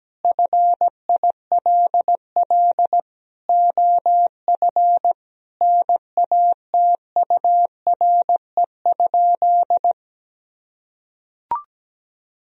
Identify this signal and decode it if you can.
Morse code, heard as a keyed tone, about 17 wpm, 700 Hz. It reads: FILL OF NATURE?
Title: Sentences from Top 1000 Words - Review Part 3 - 17wpm